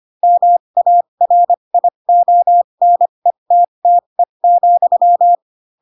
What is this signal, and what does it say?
Morse code, heard as a keyed tone, about 25 wpm, spaced as for 22 wpm, 700 Hz. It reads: MARIONETTE,
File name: Pinocchio / Pinocchio ch3 18 25 ik1hge